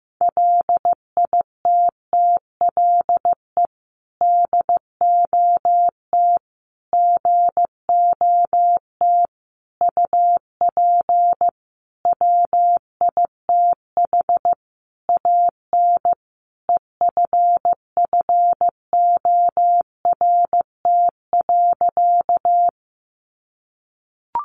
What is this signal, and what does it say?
Morse code, heard as a keyed tone, about 15 wpm, 700 Hz. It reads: LITTLE DOT GOT UP WITH AN EFFORT.
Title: Sentences from Top 1500 Words 15wpm